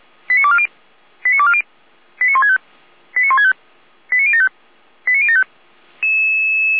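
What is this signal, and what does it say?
Five-tone selective call (ZVEI), heard as quick runs of five tones, tones of about 70 ms. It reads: 89240 89240 89165 89165 89075 89075
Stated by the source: ZVEI five-tone call received by SDR, archived by fire brigade Feuerwehr Orlamünde.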